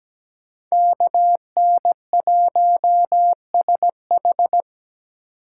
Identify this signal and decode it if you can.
Morse code, heard as a keyed tone, about 20 wpm, 700 Hz. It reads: KN1SH
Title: Speed Racing - USA 2x2 Call Signs 17wpm